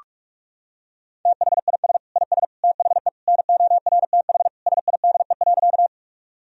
Morse code, heard as a keyed tone, about 46 wpm, 700 Hz. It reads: THIS IS THE NORTH SIDE.
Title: Sentences from Top 1000 Words - Review Part 1 - 45wpm